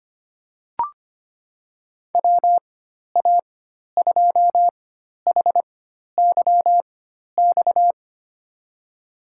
Morse code, heard as a keyed tone, about 25 wpm, spaced as for 12 wpm, 700 Hz. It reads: WA2HYX
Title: USA Call Signs - 6 Characters - 4x Character Spacing - 25wpm